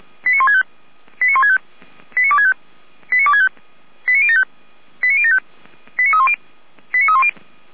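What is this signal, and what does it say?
Five-tone selective call (ZVEI), heard as quick runs of five tones, tones of about 70 ms. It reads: 89165 89165 89265 89265 89075 89075 89310 89310
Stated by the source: ZVEI five-tone call 89099 received via SDR, Feuerwehr Orlamünde.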